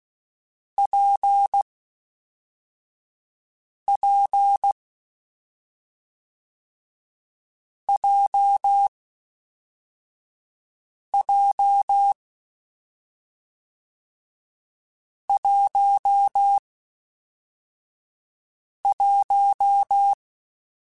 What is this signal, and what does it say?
Morse code, heard as a keyed tone, about 16 wpm, 800 Hz. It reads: P P J J 1 1
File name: Difficult Combos / Difficult Combos-mp3s / WPJ WPJ1 @5